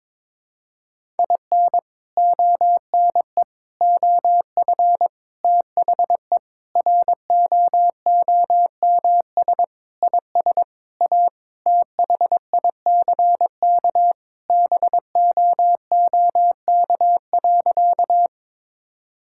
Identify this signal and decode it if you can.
Morse code, heard as a keyed tone, about 22 wpm, 700 Hz. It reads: IN ONE OF THE ROOMS IS A THICK BOOK.
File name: practice / 22wpm / Sentences with 4-Word Phrases - 22wpm